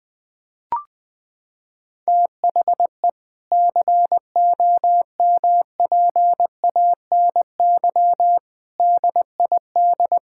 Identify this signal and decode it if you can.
Morse code, heard as a keyed tone, about 21 wpm, 700 Hz. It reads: THE COMPANY DID
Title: Sentences with 8-Word Phrases - 20wpm